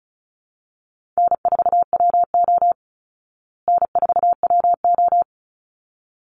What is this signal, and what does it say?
Morse code, heard as a keyed tone, about 34 wpm, 700 Hz. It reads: N4WO N4WO
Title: Call Signs Repeated 3x - 35wpm